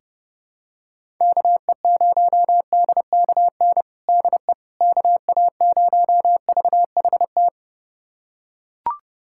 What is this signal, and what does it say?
Morse code, heard as a keyed tone, about 30 wpm, 700 Hz. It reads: KE0DKN DE KA0VHT
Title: USA Callsign DE Callsign - 6 Characters - 30wpm